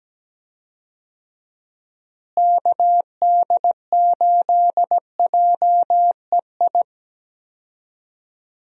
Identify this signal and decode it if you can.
Morse code, heard as a keyed tone, about 17 wpm, 700 Hz. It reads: KD8JEI
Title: USA Call Signs - 6 Characters - Repeated 3x - 17wpm